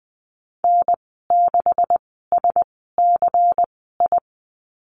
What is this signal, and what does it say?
Morse code, heard as a keyed tone, about 20 wpm, spaced as for 14 wpm, 700 Hz. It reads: N6SCI
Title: 245 Lesson 39g - Letter z - Call Signs 20wpm